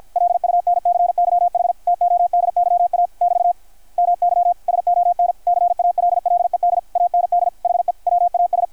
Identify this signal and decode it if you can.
Morse code, heard as a keyed tone, about 53 wpm, 660 Hz. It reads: LUNCYH TODYA <BT> MY SON CALLED AND HE WAS
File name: archived / CRSnet